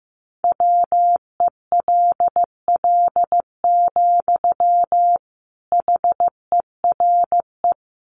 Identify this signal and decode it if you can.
Morse code, heard as a keyed tone, about 15 wpm, 700 Hz. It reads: WELL, HERE
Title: Sentences from Top 200 Words 15wpm